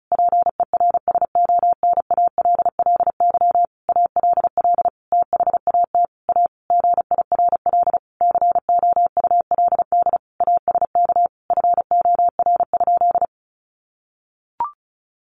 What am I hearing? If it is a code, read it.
Morse code, heard as a keyed tone, about 34 wpm, 700 Hz. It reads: PERSONALLY ALL THAT A GIRL COULD ASK FOR?
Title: Sentences from Top 1400 Words 35wpm